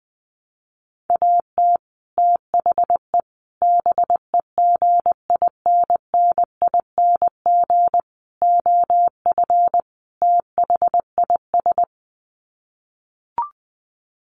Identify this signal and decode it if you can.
Morse code, heard as a keyed tone, about 20 wpm, 700 Hz. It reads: AT THE BEGINNING OF THIS
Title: Sets of 5 Words Encore 20wpm